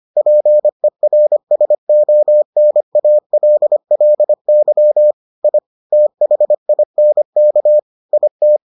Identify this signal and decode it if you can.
Morse code, heard as a keyed tone, about 25 wpm, 600 Hz. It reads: PERSONALLY I THINK IT